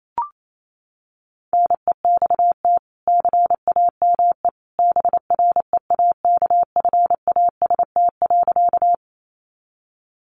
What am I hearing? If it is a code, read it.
Morse code, heard as a keyed tone, about 28 wpm, 700 Hz. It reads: NEXT CAME BREAKFAST.
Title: Sentences from Top 1500 Words 28wpm